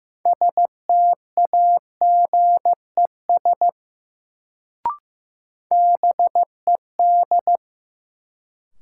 Morse code, heard as a keyed tone, about 15 wpm, 700 Hz. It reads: STAGES BED
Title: Sets of 5 Words Encore - New Word Drill - 15wpm